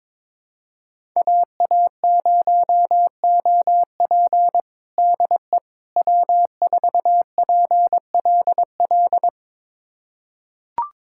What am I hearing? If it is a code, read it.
Morse code, heard as a keyed tone, about 22 wpm, 700 Hz. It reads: AA0OP DE W4PLL
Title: USA Callsign DE Callsign - 5 Characters - 22wpm